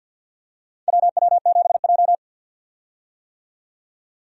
Morse code, heard as a keyed tone, about 52 wpm, 700 Hz. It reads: WW7J
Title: USA Call Signs - 4 Characters - 50wpm